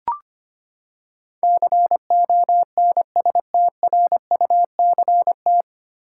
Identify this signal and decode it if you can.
Morse code, heard as a keyed tone, about 25 wpm, 700 Hz. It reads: CONSTRUCT
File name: Nine Letter Words - 25wpm